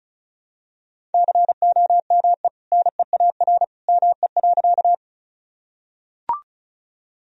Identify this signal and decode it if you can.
Morse code, heard as a keyed tone, about 36 wpm, 700 Hz. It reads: COME NEAR ME.
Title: Sentences from Top 1000 Words - Review Part 1 - 35wpm